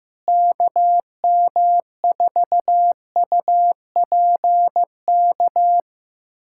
Morse code, heard as a keyed tone, about 15 wpm, 700 Hz. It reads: KM4UPK
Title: USA Callsign DE Callsign - 6 Characters - 15wpm